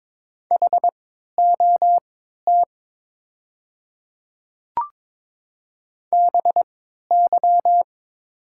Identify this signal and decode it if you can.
Morse code, heard as a keyed tone, about 22 wpm, spaced as for 12 wpm, 700 Hz. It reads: HOT BY
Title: 3x Character Spacing - Top 200 Words - 22wpm